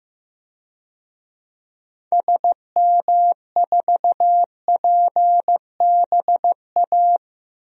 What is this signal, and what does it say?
Morse code, heard as a keyed tone, about 15 wpm, 700 Hz. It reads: SM4PBA
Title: DXCC-100 Call Signs - 6 Characters - Repeated 3x - 15wpm